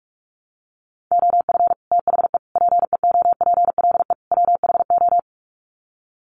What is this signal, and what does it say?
Morse code, heard as a keyed tone, about 44 wpm, 700 Hz. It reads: OF THE PEOPLE WHO
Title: Sets of 4 Words 45wpm